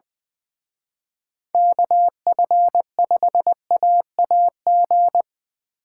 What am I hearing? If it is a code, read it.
Morse code, heard as a keyed tone, about 24 wpm, 700 Hz. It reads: KF5AAG
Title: Speed Racing - USA 2x3 Call Signs 20wpm